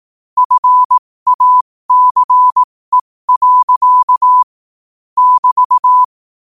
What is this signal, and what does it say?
Morse code, heard as a keyed tone, about 18 wpm, spaced as for 15 wpm, 1000 Hz. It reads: FACE. <BT>
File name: qotdfiles / 15wpm